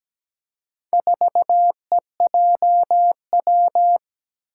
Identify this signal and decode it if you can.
Morse code, heard as a keyed tone, about 17 wpm, 700 Hz. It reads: 4EJW
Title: Four Challenging Letter-Number Combinations 17wpm